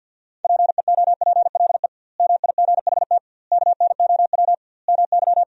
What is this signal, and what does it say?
Morse code, heard as a keyed tone, about 52 wpm, 700 Hz. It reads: PEOPLE MIGHT KNOW MY
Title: 2x Word-Spacing - Sentences from Top 500 Words - 50wpm